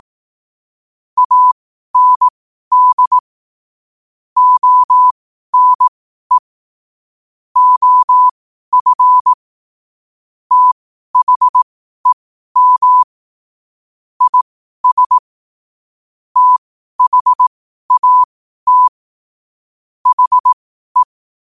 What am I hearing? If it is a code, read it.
Morse code, heard as a keyed tone, about 18 wpm, spaced as for 12 wpm, 1000 Hz. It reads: AND ONE OF THEM IS THAT HE
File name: qotdfiles / 12wpm